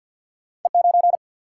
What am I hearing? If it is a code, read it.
Morse code, heard as a keyed tone, about 47 wpm, 700 Hz. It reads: E9
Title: Single-Letter Single-Number 50wpm